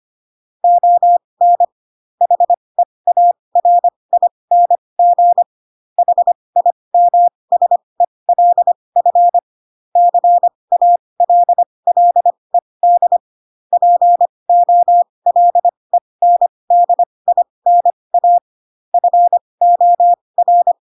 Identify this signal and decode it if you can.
Morse code, heard as a keyed tone, about 25 wpm, spaced as for 20 wpm, 700 Hz. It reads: ON HEARING HIMSELF CALLED POLENDINA FOR
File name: Pinocchio / Pinocchio ch2 15 25 ik1hge